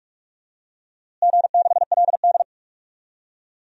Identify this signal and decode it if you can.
Morse code, heard as a keyed tone, about 41 wpm, 700 Hz. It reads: G6LD
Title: DXCC-100 Call Signs - 4 Characters - Repeated 2x - 45wpm